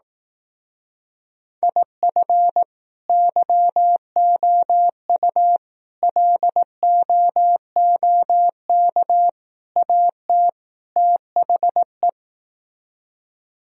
Morse code, heard as a keyed tone, about 18 wpm, 700 Hz. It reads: IF YOU LOOK AT THE
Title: Speed-Racing - Sets of 5 Words 15wpm